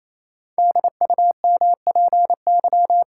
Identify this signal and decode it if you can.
Morse code, heard as a keyed tone, about 28 wpm, 700 Hz. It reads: DUMPY
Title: Pangram Challenge v2 - 28wpm